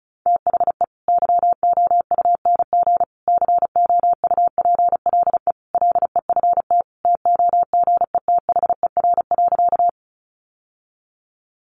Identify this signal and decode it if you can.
Morse code, heard as a keyed tone, about 34 wpm, 700 Hz. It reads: THE YOUNG COUPLE LEFT TOGETHER.
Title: Sentences from Top 1100 Words 35wpm